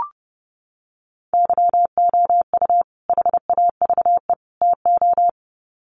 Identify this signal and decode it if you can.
Morse code, heard as a keyed tone, about 28 wpm, 700 Hz. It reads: YOU HAVE TO
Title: Sets of 3 Words 30wpm